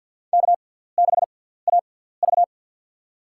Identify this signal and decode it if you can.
Morse code, heard as a keyed tone, about 50 wpm, spaced as for 17 wpm, 690 Hz. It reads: K6AV
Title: USA Call Signs - 4 Characters - 6x Character Spacing - 50wpm